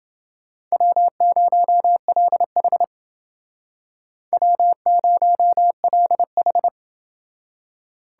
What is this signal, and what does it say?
Morse code, heard as a keyed tone, about 28 wpm, 700 Hz. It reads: W0LH W0LH